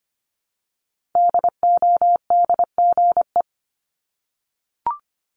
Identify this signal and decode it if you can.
Morse code, heard as a keyed tone, about 25 wpm, 700 Hz. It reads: DODGE
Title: Multi-letter Phonemes Repeated 3x 25wpm